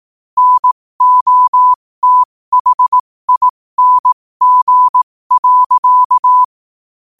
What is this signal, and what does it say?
Morse code, heard as a keyed tone, about 18 wpm, spaced as for 15 wpm, 1000 Hz. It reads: NOTHING.
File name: qotdfiles / 15wpm